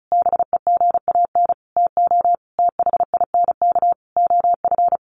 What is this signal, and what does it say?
Morse code, heard as a keyed tone, about 34 wpm, 700 Hz. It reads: BEGAN TO THINK OF
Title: Sentences from Top 800 Words 35wpm